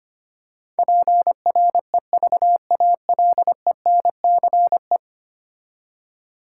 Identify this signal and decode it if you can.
Morse code, heard as a keyed tone, about 25 wpm, 700 Hz. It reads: PREVALENCE
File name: 10 Letter Words - First 3 Letters Given - 25wpm